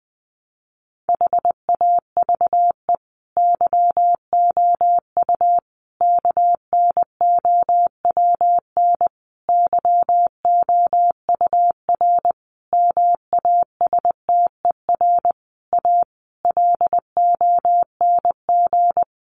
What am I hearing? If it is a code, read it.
Morse code, heard as a keyed tone, about 20 wpm, 700 Hz. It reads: HAVE YOU KNOWN YOUR MASTER A LONG
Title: Sentences from Top 1300 Words 20wpm